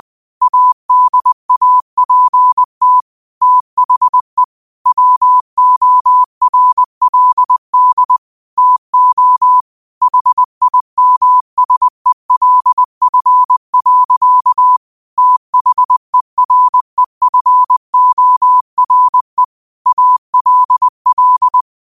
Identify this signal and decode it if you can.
Morse code, heard as a keyed tone, about 20 wpm, 1000 Hz. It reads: ADAPT THE WORLD TO HIMSELF. THEREFORE ALL